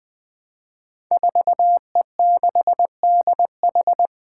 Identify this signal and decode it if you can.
Morse code, heard as a keyed tone, about 20 wpm, 700 Hz. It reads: 4E6DH
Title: Five Challenging Letter-Number Combinations 20wpm